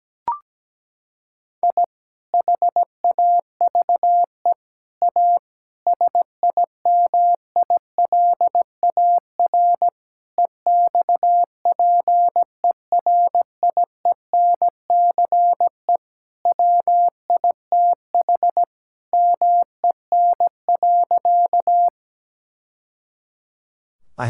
Morse code, heard as a keyed tone, about 17 wpm, 700 Hz. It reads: I HAVE A SIMILAR EXPERIENCE WITH MEN.
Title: Sentences from Top 1000 Words - Review Part 5 - 17wpm